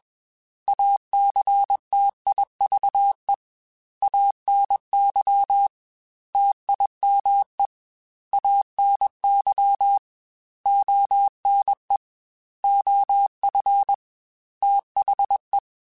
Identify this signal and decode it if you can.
Morse code, heard as a keyed tone, about 21 wpm, 800 Hz. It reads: ACTIVE ANY TIME ANY ONE OF THE
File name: Technical Subjects / Technical Subjects-mp3s / OR Gate @20